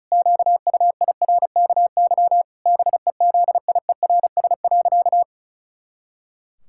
Morse code, heard as a keyed tone, about 34 wpm, 700 Hz. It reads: QUIRKY BEZIERS.
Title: Pangram Challenge v2 - 35wpm